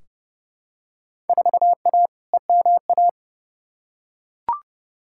Morse code, heard as a keyed tone, about 30 wpm, 700 Hz. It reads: 4A EMA
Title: Field Day Exchanges - 30wpm